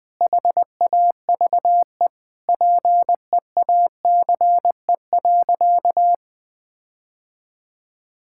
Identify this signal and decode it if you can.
Morse code, heard as a keyed tone, about 20 wpm, 700 Hz. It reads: HAVE PEACE.
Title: Sentences with 9-Word Phrases - 20wpm